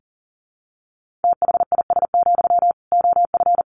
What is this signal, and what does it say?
Morse code, heard as a keyed tone, about 40 wpm, 700 Hz. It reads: THIS, OF
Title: Sentences from Top 1400 Words 40wpm